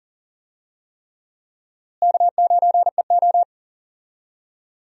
Morse code, heard as a keyed tone, about 37 wpm, 700 Hz. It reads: K9EO
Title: USA Call Signs - 4 Characters - Repeated 3x - 40wpm